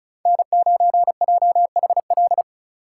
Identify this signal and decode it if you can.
Morse code, heard as a keyed tone, about 36 wpm, 700 Hz. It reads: N9JHL
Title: K1USN SST QSO Practice - 35wpm